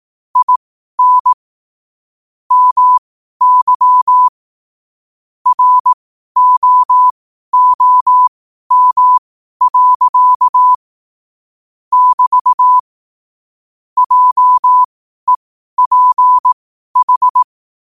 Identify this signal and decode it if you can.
Morse code, heard as a keyed tone, about 18 wpm, spaced as for 12 wpm, 1000 Hz. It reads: IN MY ROOM. <BT> JEPH
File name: qotdfiles / 12wpm